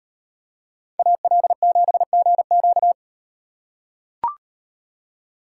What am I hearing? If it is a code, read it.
Morse code, heard as a keyed tone, about 38 wpm, 700 Hz. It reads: AL7GQ